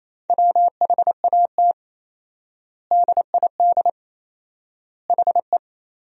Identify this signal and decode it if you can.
Morse code, heard as a keyed tone, about 28 wpm, 700 Hz. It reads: WHAT DID HE
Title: 4x Word-Spacing - Sentences from Top 100 Words - 28wpm